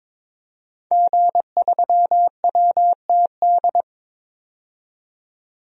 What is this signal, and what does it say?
Morse code, heard as a keyed tone, about 22 wpm, 700 Hz. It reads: G3WTD
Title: DXCC-100 Call Signs - 5 Characters - 22wpm